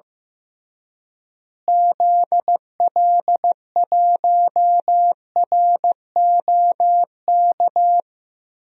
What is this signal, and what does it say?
Morse code, heard as a keyed tone, about 15 wpm, 700 Hz. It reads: ZL1ROK
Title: DXCC-100 Call Signs - 6 Characters - Repeated 3x - 15wpm